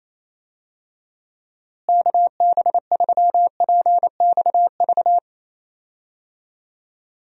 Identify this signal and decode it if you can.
Morse code, heard as a keyed tone, about 28 wpm, 700 Hz. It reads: KB3PXV
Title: USA Call Signs - 6 Characters - Repeated 3x - 28wpm